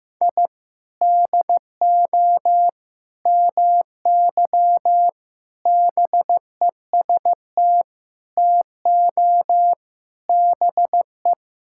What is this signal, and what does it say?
Morse code, heard as a keyed tone, about 14 wpm, 700 Hz. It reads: I DO MY BEST TO BE
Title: Sentences from Top 1000 Words - Review Part 6 - 15wpm